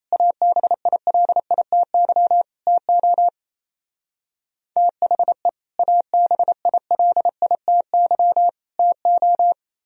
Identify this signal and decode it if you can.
Morse code, heard as a keyed tone, about 34 wpm, 700 Hz. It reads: ABILITY TO THE ABILITY TO
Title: Speed-Racing - Sets of 3 Words 28wpm